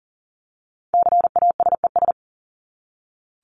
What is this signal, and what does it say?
Morse code, heard as a keyed tone, about 40 wpm, 700 Hz. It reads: CASES
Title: ES:S Suffix - Top 100 Words with Base Words 40wpm